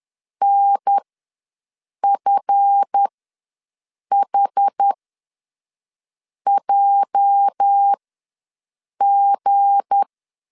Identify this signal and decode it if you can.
Morse code, heard as a keyed tone, about 10 wpm, spaced as for 5 wpm, 800 Hz. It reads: NFHJG
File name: LEZIONI / 8 YJ45 / 8 YJ ws 10wpm 4043